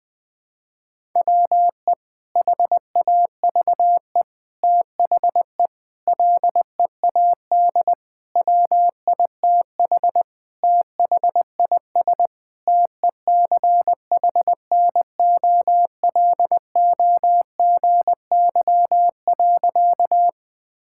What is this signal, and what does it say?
Morse code, heard as a keyed tone, about 21 wpm, 700 Hz. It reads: WE HAVE THE LEAD WITH THIS TECHNOLOGY.
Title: Sentences from Top 1000 Words - Review Part 5 - 20wpm